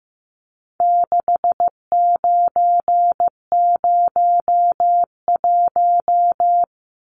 Morse code, heard as a keyed tone, about 15 wpm, 700 Hz. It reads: 6901
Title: Numbers - 4-Digit 15wpm